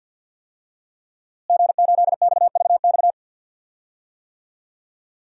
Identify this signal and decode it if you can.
Morse code, heard as a keyed tone, about 51 wpm, 700 Hz. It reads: G8XVX
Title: DXCC-100 Call Signs - 5 Characters - Repeated 2x - 50wpm